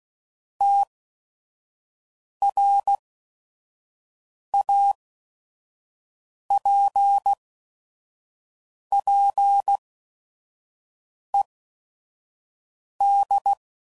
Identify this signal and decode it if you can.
Morse code, heard as a keyed tone, about 16 wpm, spaced as for 5 wpm, 800 Hz. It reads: TRAPPED
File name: CW Code Groups @5WPM / QSOs / kc7sac